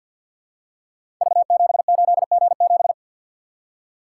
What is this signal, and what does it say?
Morse code, heard as a keyed tone, about 46 wpm, 700 Hz. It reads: V78G7
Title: Code Groups - 5 Letter:Numbers - 50wpm